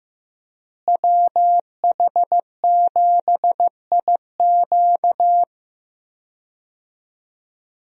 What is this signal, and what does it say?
Morse code, heard as a keyed tone, about 15 wpm, 700 Hz. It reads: WH7IQ
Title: USA Call Signs - 5 Characters - Repeated 3x - 15wpm